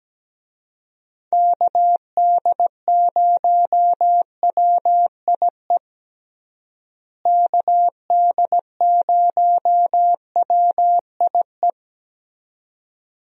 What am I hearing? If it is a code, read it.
Morse code, heard as a keyed tone, about 17 wpm, 700 Hz. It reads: KD0WIE KD0WIE
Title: USA Call Signs - 6 Characters - Repeated 2x - 17wpm